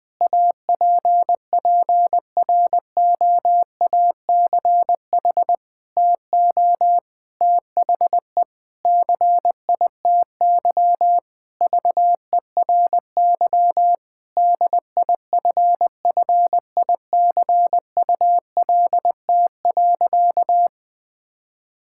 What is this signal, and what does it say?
Morse code, heard as a keyed tone, about 20 wpm, 700 Hz. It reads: APPROACH TO THE CITY VERY DIFFICULT.
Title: Sentences from Top 1000 Words - Review Part 6 - 20wpm